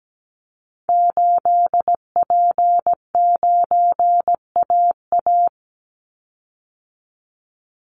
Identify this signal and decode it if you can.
Morse code, heard as a keyed tone, about 17 wpm, 700 Hz. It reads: 8P9AA
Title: Call Signs - 17wpm